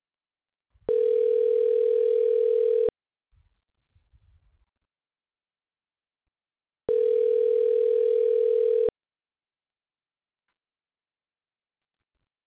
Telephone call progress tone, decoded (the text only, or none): ringback tone